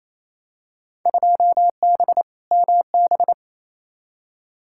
Morse code, heard as a keyed tone, about 28 wpm, 700 Hz. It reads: 2B MB
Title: Field Day Exchanges Encore - 28wpm